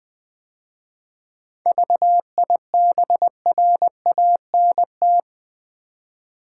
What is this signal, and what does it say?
Morse code, heard as a keyed tone, about 20 wpm, 700 Hz. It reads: VIBRANT
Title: 7 Letter Words - First 3 Letters Given - 20wpm